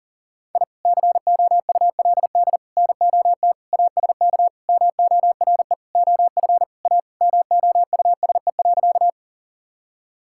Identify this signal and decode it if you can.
Morse code, heard as a keyed tone, about 37 wpm, 700 Hz. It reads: I COULD NOT ASK MORE OF A MOUSE.
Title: Sentences from Top 1000 Words - Review Part 5 - 40wpm